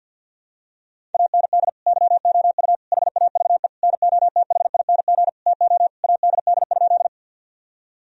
Morse code, heard as a keyed tone, about 51 wpm, 700 Hz. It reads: AND YOU HAVE NOTHING TO ADD?
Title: Sentences from Top 1000 Words - Review Part 1 - 50wpm